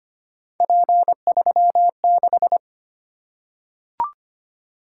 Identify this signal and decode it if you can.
Morse code, heard as a keyed tone, about 25 wpm, 700 Hz. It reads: P36
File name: Three Letter-Number Combinations - 25wpm